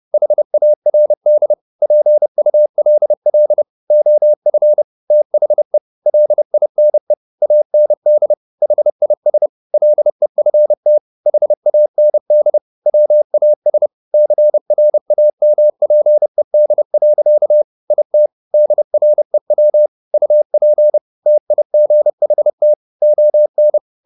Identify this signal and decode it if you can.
Morse code, heard as a keyed tone, about 30 wpm, 600 Hz. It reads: HARD PULL OF THE LINE AND HIS LEFT HAND WAS CRAMPED. IT DREW UP TIGHT ON
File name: the old man and the sea / the old man and the sea 1 30wpm